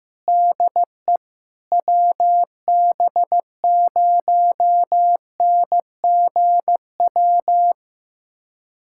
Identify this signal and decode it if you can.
Morse code, heard as a keyed tone, about 15 wpm, 700 Hz. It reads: DE WB0NGW
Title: USA Callsign DE Callsign - 6 Characters - 15wpm